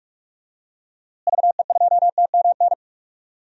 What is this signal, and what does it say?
Morse code, heard as a keyed tone, about 47 wpm, 700 Hz. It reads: VE2TMN
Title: DXCC-100 Call Signs - 6 Characters - Repeated 2x - 45wpm